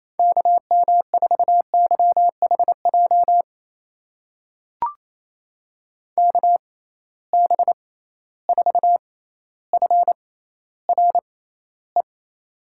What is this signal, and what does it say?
Morse code, heard as a keyed tone, about 28 wpm, spaced as for 14 wpm, 700 Hz. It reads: KM4YHJ KB4FRE
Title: USA Call Signs - 6 Characters - 6x Character Spacing - 28wpm